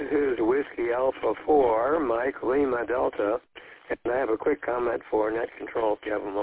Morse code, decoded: none

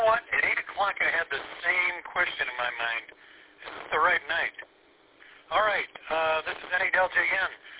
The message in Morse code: none

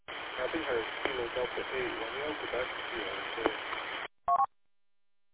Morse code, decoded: none

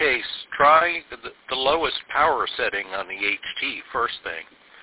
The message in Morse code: none

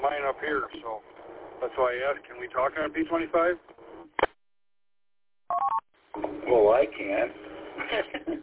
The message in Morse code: none

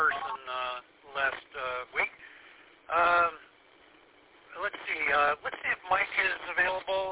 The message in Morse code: none

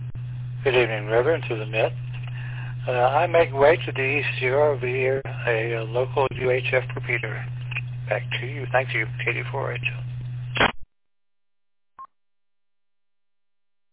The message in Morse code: none

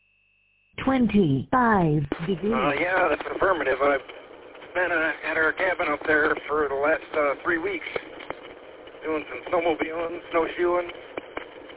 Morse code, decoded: none

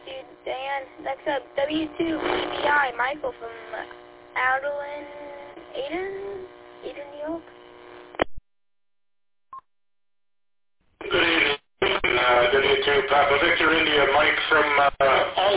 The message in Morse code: none

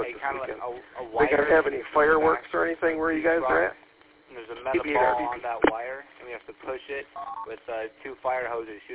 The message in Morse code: none

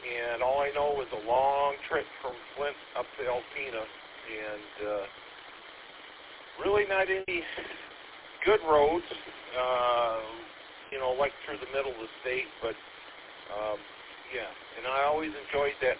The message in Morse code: none

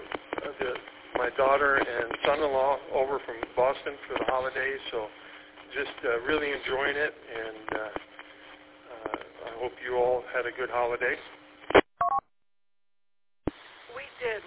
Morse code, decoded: none

